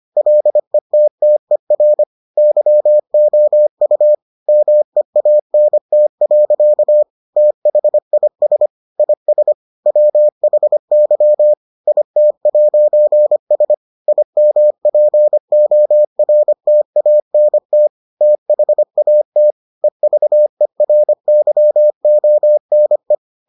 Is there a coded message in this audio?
Yes: Morse code from 0.2 to 23.1 s, heard as a keyed tone, about 25 wpm, 600 Hz. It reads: LETTER YOU MEANT. THIS IS WHY IT'S IMPORTANT THAT EVERYONE